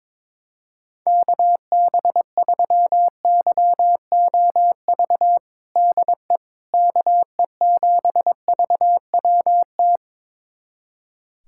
Morse code, heard as a keyed tone, about 22 wpm, 700 Hz. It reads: KB3YOV DE KE7VWT
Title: USA Callsign DE Callsign - 6 Characters - 22wpm